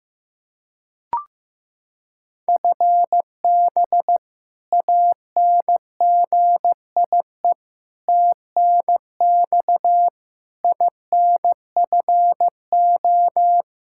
Morse code, heard as a keyed tone, about 15 wpm, 700 Hz. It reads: FB ANGIE TNX INFO
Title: K1USN SST QSO Practice 15wpm